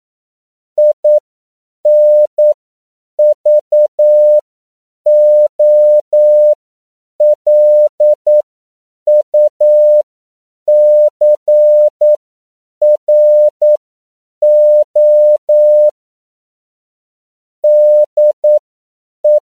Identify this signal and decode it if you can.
Morse code, heard as a keyed tone, about 9 wpm, spaced as for 7 wpm, 600 Hz. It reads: INVOLUCRO DE
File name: CW / 10PPM